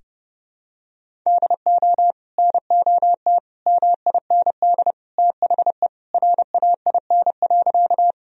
Morse code, heard as a keyed tone, about 30 wpm, 700 Hz. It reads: DO NOT MIND THE RAIN.
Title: Sentences from Top 1000 Words - Review Part 1 - 30wpm